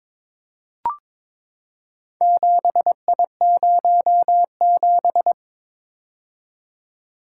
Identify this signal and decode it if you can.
Morse code, heard as a keyed tone, about 22 wpm, 700 Hz. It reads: 7I07
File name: practice / 22wpm / Four Letter-Number Combinations - 22wpm